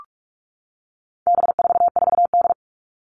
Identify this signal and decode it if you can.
Morse code, heard as a keyed tone, about 44 wpm, 700 Hz. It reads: B44D
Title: All Cut Numbers - 4-Digit 45wpm